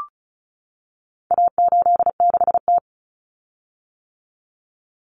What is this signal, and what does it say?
Morse code, heard as a keyed tone, about 34 wpm, 700 Hz. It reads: A86T
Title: Common Cut Numbers - 4-Digit 35wpm